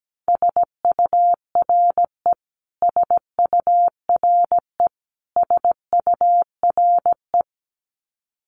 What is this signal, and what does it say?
Morse code, heard as a keyed tone, about 17 wpm, 700 Hz. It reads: SURE SURE SURE
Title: Top 500 Words Repeated 17wpm